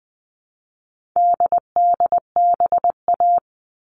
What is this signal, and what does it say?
Morse code, heard as a keyed tone, about 20 wpm, 700 Hz. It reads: DDBA
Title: All Cut Numbers - 4-Digit 20wpm